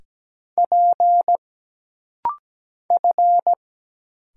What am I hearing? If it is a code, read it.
Morse code, heard as a keyed tone, about 17 wpm, 700 Hz. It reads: P F
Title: Single Letters - Rapid-Fire 17wpm